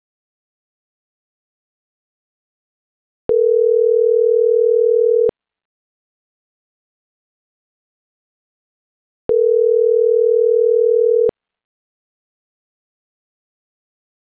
Ringback tone (North America), 440 + 480 Hz.